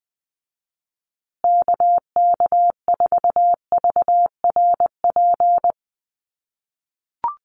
Morse code, heard as a keyed tone, about 20 wpm, 700 Hz. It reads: KK4VRP
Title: USA Call Signs Repeated 3x 20wpm